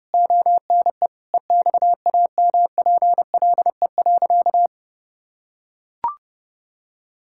Morse code, heard as a keyed tone, about 28 wpm, 700 Hz. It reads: ONE EXAMPLE.
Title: Sentences from Top 1000 Words - Review Part 6 - 30wpm